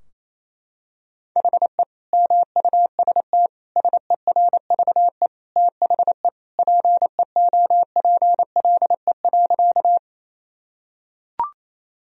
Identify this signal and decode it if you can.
Morse code, heard as a keyed tone, about 28 wpm, 700 Hz. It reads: HE MUST SERVE THE PEOPLE.
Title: Sentences from Top 1000 Words - Review Part 2 - 28wpm